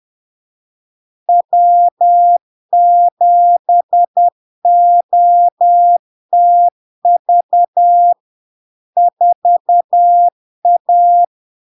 Morse code, heard as a keyed tone, about 10 wpm, 700 Hz. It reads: W7OTV 4A
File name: LARK-FD-2022-Log-in-CW-10wpm-700-Hz